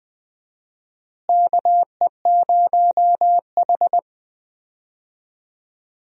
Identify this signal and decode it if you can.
Morse code, heard as a keyed tone, about 20 wpm, 700 Hz. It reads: KE0H
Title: USA Call Signs - 4 Characters - Repeated 2x - 20wpm